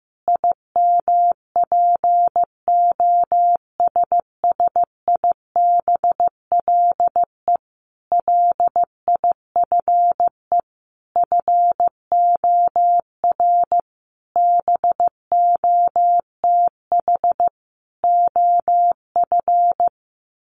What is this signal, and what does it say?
Morse code, heard as a keyed tone, about 15 wpm, 700 Hz. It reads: IMPOSSIBLE LIFE FOR BOTH OF
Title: Sentences from Top 1100 Words 15wpm